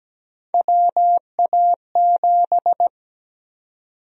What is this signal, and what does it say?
Morse code, heard as a keyed tone, about 17 wpm, 700 Hz. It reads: WA7
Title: USA Call Sign Prefixes - 17wpm